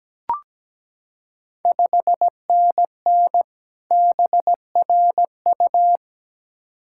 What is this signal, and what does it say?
Morse code, heard as a keyed tone, about 17 wpm, 700 Hz. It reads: 5NN BRU
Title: Ontario QSO Party Exchange 17wpm